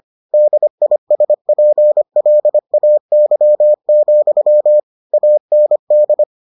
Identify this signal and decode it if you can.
Morse code, heard as a keyed tone, about 25 wpm, 600 Hz. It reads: DISPLAY, AND